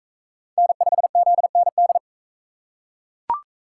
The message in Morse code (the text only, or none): N5ZND